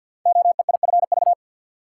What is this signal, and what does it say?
Morse code, heard as a keyed tone, about 52 wpm, 700 Hz. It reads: OEIFV
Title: Code Groups - 5 Letters - 50wpm